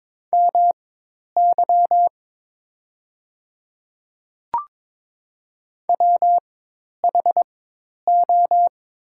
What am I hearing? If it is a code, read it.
Morse code, heard as a keyed tone, about 22 wpm, spaced as for 10 wpm, 700 Hz. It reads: MY WHO
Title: 4x Character Spacing - Top 100 Words - 22wpm